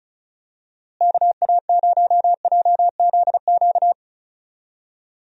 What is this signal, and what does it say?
Morse code, heard as a keyed tone, about 36 wpm, 700 Hz. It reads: KA0JZQ
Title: USA Call Signs - 6 Characters - Repeated 3x - 35wpm